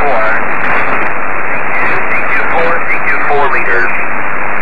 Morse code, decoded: none